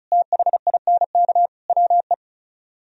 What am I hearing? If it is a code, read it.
Morse code, heard as a keyed tone, about 43 wpm, 700 Hz. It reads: THINK WE
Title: Speed-Racing - Sets of 3 Words 35wpm